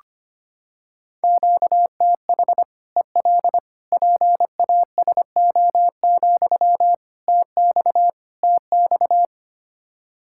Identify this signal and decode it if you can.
Morse code, heard as a keyed tone, about 25 wpm, 700 Hz. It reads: QTH EL PASO, TX TX
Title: Practice QSO - Exchange 1 - Beginning - 25wpm